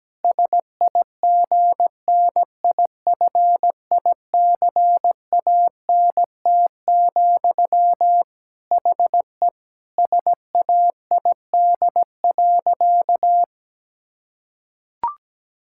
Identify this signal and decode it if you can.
Morse code, heard as a keyed tone, about 17 wpm, 700 Hz. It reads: SIGNIFICANT, HE SAID.
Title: Sentences from Top 1000 Words - Review Part 6 - 17wpm